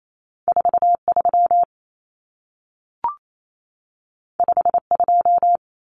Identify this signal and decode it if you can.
Morse code, heard as a keyed tone, about 28 wpm, 700 Hz. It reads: 43 52
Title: Numbers - 2-Digit 28wpm